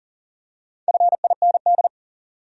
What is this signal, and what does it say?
Morse code, heard as a keyed tone, about 40 wpm, 700 Hz. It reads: FIND
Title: Top 100 Words 40wpm